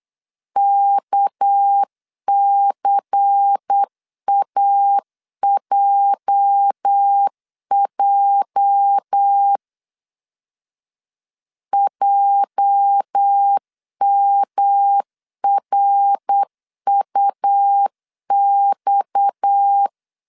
Morse code, heard as a keyed tone, about 8 wpm, 800 Hz. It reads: KCAJJ JMRUX
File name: LEZIONI / 8 YJ45 / 8 YJ 8wpm 3313